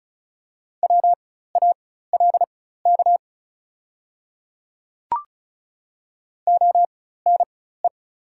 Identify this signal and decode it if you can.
Morse code, heard as a keyed tone, about 34 wpm, spaced as for 16 wpm, 690 Hz. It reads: WALK ONE
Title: 4x Character Spacing - Top 300 Words - 35wpm